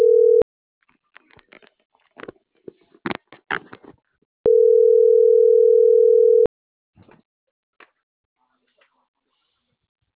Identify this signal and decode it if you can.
Ringback tone (North America), 440 + 480 Hz.